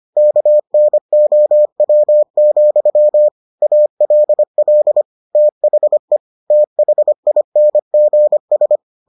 Morse code, heard as a keyed tone, about 25 wpm, 600 Hz. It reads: KNOW, ALL THE THINGS